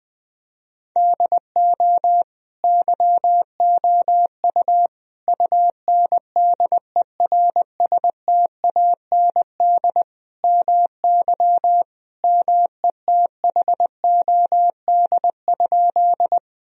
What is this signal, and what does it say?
Morse code, heard as a keyed tone, about 21 wpm, 700 Hz. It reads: DO YOU UNDERSTAND MY METHOD?
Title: Sentences from Top 1000 Words - Review Part 3 - 20wpm